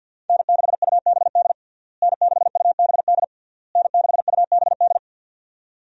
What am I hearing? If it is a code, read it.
Morse code, heard as a keyed tone, about 52 wpm, 700 Hz. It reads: N6UBD N6UBD N6UBD
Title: USA Call Signs - 5 Characters - Repeated 3x - 50wpm